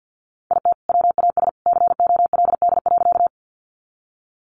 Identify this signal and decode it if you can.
Morse code, heard as a keyed tone, about 50 wpm, 700 Hz. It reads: IT WAS COLD.
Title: Sentences from Top 400 Words 50wpm